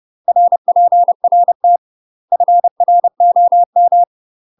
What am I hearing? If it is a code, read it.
Morse code, heard as a keyed tone, about 30 wpm, 700 Hz. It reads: RPRT FROM
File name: UK MASTER QSOs TRAINING / 30WPM / UK MASTER QSOs 0030